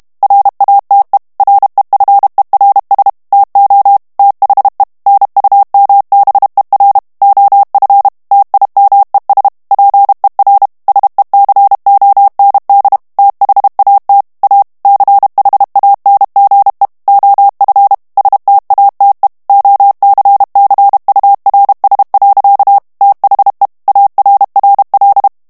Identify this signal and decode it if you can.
Morse code, heard as a keyed tone, about 32 wpm, 800 Hz. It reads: RATE REFERS TO THE NUMBER OF TIMES PER SECOND THAT A CHANGE OF STATE OCCURS. THE ARRL
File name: morse mp3 / ANGLAIS texte clair 3 vitesse 30 mots minute